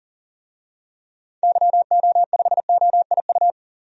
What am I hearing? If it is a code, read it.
Morse code, heard as a keyed tone, about 40 wpm, 700 Hz. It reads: YO5OIU
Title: DXCC-100 Call Signs - 6 Characters - Repeated 3x - 40wpm